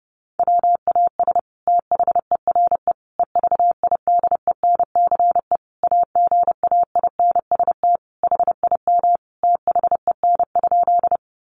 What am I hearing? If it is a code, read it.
Morse code, heard as a keyed tone, about 28 wpm, 700 Hz. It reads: WAS THERE EVIDENCE AGAINST HIM THEN?
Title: Sentences from Top 1300 Words 30wpm